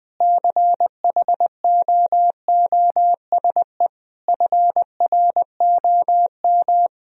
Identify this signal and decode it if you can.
Morse code, heard as a keyed tone, about 19 wpm, 700 Hz. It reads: CHOOSE FROM
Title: Sentences from Top 1000 Words - Review Part 4 - 20wpm